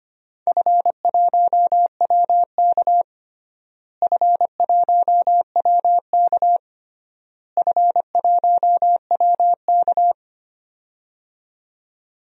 Morse code, heard as a keyed tone, about 25 wpm, 700 Hz. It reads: F1WK F1WK F1WK
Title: DXCC-100 Call Signs - 4 Characters - Repeated 3x - 25wpm